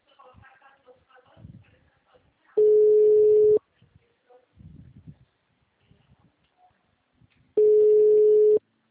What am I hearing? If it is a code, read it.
Ringback tone (Europe, 425 Hz).
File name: out- -2027- - - .503